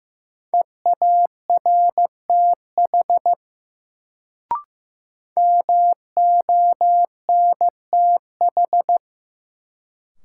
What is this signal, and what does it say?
Morse code, heard as a keyed tone, about 15 wpm, 700 Hz. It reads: EARTH MONTH
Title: Sets of 5 Words Encore - New Word Drill - 15wpm